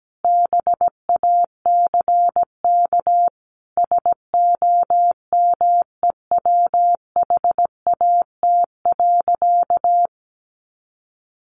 Morse code, heard as a keyed tone, about 17 wpm, 700 Hz. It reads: BACK SOMEWHAT.
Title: Sentences from Top 1300 Words 17wpm